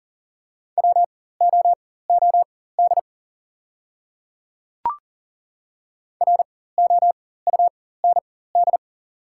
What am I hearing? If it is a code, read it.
Morse code, heard as a keyed tone, about 38 wpm, spaced as for 18 wpm, 700 Hz. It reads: WOOD ROUND
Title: 4x Character Spacing - Top 300 Words - 40wpm